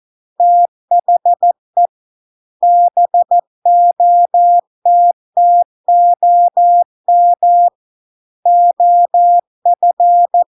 Morse code, heard as a keyed tone, about 14 wpm, 700 Hz. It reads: THE BOTTOM OF